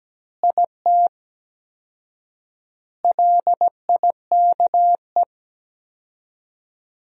Morse code, heard as a keyed tone, about 17 wpm, 700 Hz. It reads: IT LIKE
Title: 4x Word-Spacing - Sentences from Top 300 Words - 17wpm